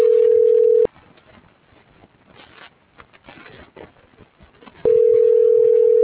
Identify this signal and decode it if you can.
Ringback tone (North America), 440 + 480 Hz.